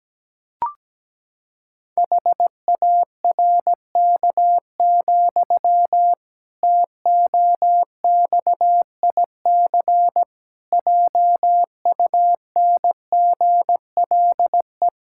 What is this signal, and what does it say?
Morse code, heard as a keyed tone, about 17 wpm, 700 Hz. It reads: HARK, TOXIC JUNGLE
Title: Pangram Challenge v2 - 17wpm